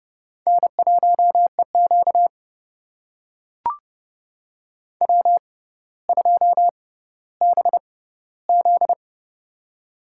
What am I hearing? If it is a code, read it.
Morse code, heard as a keyed tone, about 29 wpm, spaced as for 15 wpm, 700 Hz. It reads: N1EQ W2BZ
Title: USA Call Signs - 4 Characters - 6x Character Spacing - 30wpm